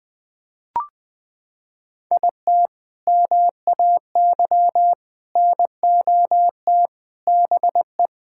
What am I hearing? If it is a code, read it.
Morse code, heard as a keyed tone, about 20 wpm, 700 Hz. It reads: IT MAY NOT BE
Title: Sentences with 3-Word Phrases - 20wpm